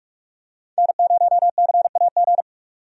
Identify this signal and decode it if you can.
Morse code, heard as a keyed tone, about 44 wpm, 700 Hz. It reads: N0CAG
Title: USA Call Signs - 5 Characters - 45wpm